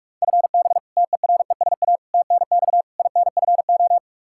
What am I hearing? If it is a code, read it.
Morse code, heard as a keyed tone, about 46 wpm, 700 Hz. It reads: FB TERESA TNX INFO
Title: Practice QSO - Exchange 3 - More Info - 45wpm